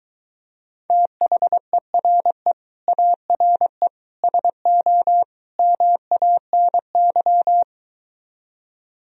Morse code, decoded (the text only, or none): THERE ARE SO MANY